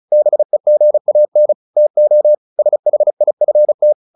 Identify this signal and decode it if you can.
Morse code, heard as a keyed tone, about 34 wpm, 600 Hz. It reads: BEGAN TO SHIFT
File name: the old man and the sea / the old man and the sea 1 35wpm